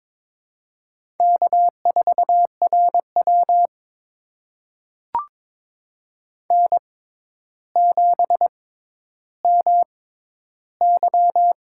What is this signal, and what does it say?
Morse code, heard as a keyed tone, about 22 wpm, spaced as for 11 wpm, 700 Hz. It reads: K4RW N7MY
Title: USA Call Signs - 6x Character Spacing - 22wpm